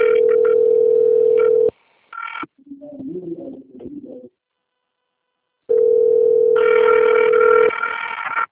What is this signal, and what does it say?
Ringback tone (North America), 440 + 480 Hz.
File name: rg-1600-unknown- - - .491